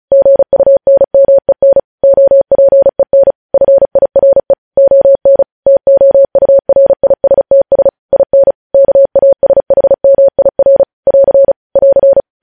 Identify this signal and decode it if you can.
Morse code, heard as a keyed tone, about 36 wpm, 550 Hz. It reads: GUNMEN OPEN FIRE ON TOURISTS IN KASHMIR <AR> <AR>
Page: news headlines in morse code at 35 wpm